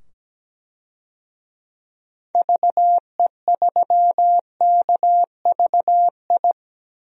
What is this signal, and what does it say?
Morse code, heard as a keyed tone, about 17 wpm, 700 Hz. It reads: VE3KVI
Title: DXCC-100 Call Signs - 6 Characters - 17wpm